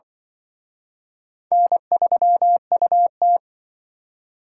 Morse code, heard as a keyed tone, about 23 wpm, 700 Hz. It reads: N3UT
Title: Speed Racing - USA 1x2 Call Signs 20wpm